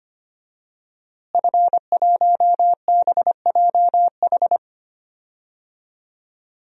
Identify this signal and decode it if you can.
Morse code, heard as a keyed tone, about 25 wpm, 700 Hz. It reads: F1BJH
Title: DXCC-100 Call Signs - 5 Characters - Repeated 2x - 25wpm